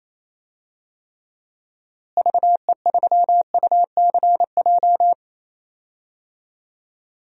Morse code, heard as a keyed tone, about 28 wpm, 700 Hz. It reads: VE3UCJ
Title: DXCC-100 Call Signs - 6 Characters - Repeated 3x - 28wpm